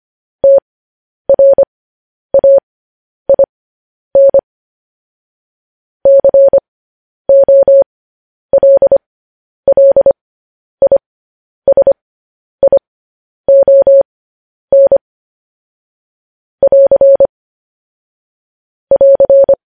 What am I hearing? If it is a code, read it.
Morse code, heard as a keyed tone, about 25 wpm, spaced as for 10 wpm, 550 Hz. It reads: TRAIN COLLISION <AR> <AR>
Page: News Headlines in Morse Code at 10 WPM